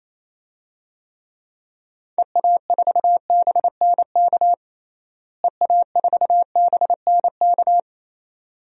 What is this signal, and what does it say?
Morse code, heard as a keyed tone, about 28 wpm, 700 Hz. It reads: EA4BNK EA4BNK
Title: DXCC-100 Call Signs - 6 Characters - Repeated 2x - 28wpm